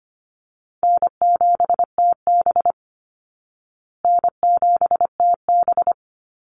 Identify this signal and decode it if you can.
Morse code, heard as a keyed tone, about 25 wpm, 700 Hz. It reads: N7TB N7TB
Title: Easy Call Signs Repeated 3x - 25wpm